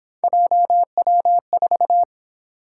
Morse code, heard as a keyed tone, about 26 wpm, 700 Hz. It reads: JW4
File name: Speed-Racing - Three Letter-Number Combinations - 22wpm